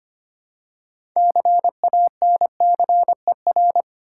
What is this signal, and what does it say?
Morse code, heard as a keyed tone, about 25 wpm, 700 Hz. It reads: CANCER
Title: Six Letter Words - 25wpm